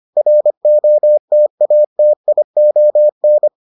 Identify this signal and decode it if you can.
Morse code, heard as a keyed tone, about 25 wpm, 600 Hz. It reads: ROTATION